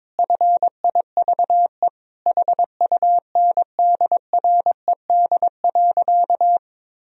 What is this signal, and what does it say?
Morse code, heard as a keyed tone, about 22 wpm, 700 Hz. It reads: FIVE HUNDRED.
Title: Sentences from Top 1000 Words - Review Part 2 - 22wpm